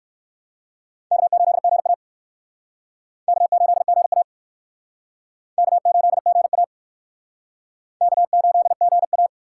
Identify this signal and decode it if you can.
Morse code, heard as a keyed tone, about 56 wpm, 700 Hz. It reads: K8GA K8GA K8GA K8GA
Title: Speed Racing - USA 1x2 Call Signs 45wpm